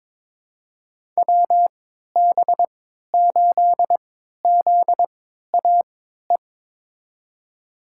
Morse code, heard as a keyed tone, about 22 wpm, spaced as for 12 wpm, 700 Hz. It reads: WB8ZAE
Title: USA Call Signs - 6 Characters - 3x Character Spacing - 22wpm